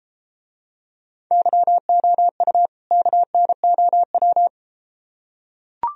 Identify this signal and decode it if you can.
Morse code, heard as a keyed tone, about 33 wpm, 700 Hz. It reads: YOU KNOW